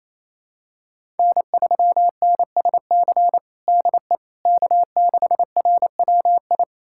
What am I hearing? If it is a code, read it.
Morse code, heard as a keyed tone, about 28 wpm, 700 Hz. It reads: N3NSC DE K6RWI
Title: USA Callsign DE Callsign - 5 Characters - 28wpm